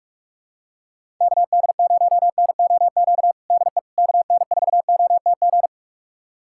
Morse code, heard as a keyed tone, about 42 wpm, 700 Hz. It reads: KD0NOQ DE KN4OTG
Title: USA Callsign DE Callsign - 6 Characters - 45wpm